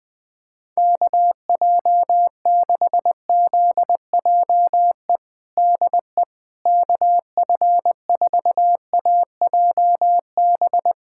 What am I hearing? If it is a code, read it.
Morse code, heard as a keyed tone, about 20 wpm, 700 Hz. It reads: KJ6ZJE DE KF4AJB
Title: USA Callsign DE Callsign 20wpm